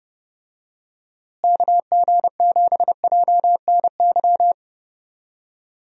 Morse code, heard as a keyed tone, about 30 wpm, 700 Hz. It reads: KG7JNY
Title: USA Call Signs - 6 Characters - Repeated 2x - 30wpm